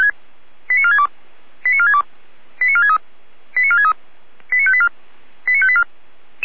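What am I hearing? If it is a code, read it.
Five-tone selective call (ZVEI), heard as quick runs of five tones, tones of about 70 ms. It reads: 89462 89462 89463 89463 89584 89584
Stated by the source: ZVEI five-tone selective call received on an SDR.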